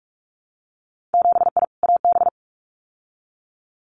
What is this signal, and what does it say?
Morse code, heard as a keyed tone, about 46 wpm, 700 Hz. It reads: 7I AB
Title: Winter Field Day Exchanges Encore 45wpm